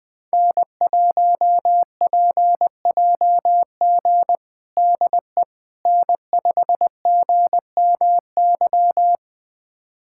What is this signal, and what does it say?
Morse code, heard as a keyed tone, about 19 wpm, 700 Hz. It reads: N1PJG DE N5GMY
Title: USA Callsign DE Callsign - 5 Characters - 20wpm